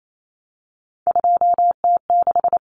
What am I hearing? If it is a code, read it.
Morse code, heard as a keyed tone, about 28 wpm, 700 Hz. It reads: 2T6
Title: Common Cut Numbers - 3-Digit 28wpm